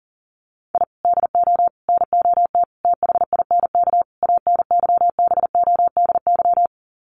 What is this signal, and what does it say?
Morse code, heard as a keyed tone, about 40 wpm, 700 Hz. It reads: I DO NOT THINK ANYBODY
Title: Sets of 5 Words 40wpm